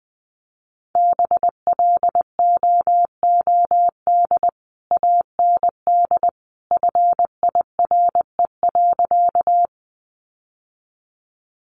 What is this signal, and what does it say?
Morse code, heard as a keyed tone, about 20 wpm, 700 Hz. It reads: BLOOD AND FIRE.
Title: Sentences from Top 1100 Words 20wpm